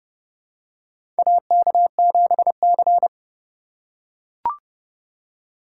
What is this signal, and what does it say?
Morse code, heard as a keyed tone, about 30 wpm, 700 Hz. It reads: AK7C
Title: USA Call Signs - 4 Characters - 8x Character Spacing - 30wpm